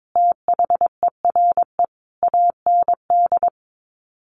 Morse code, heard as a keyed tone, about 22 wpm, 700 Hz. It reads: THERE AND
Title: Sets of 5 Words Encore2 22wpm